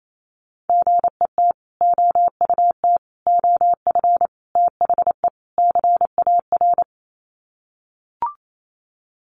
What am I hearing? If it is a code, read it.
Morse code, heard as a keyed tone, about 28 wpm, 700 Hz. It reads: GET OUT OF THE CAR
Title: Sets of 5 Words Encore2 28wpm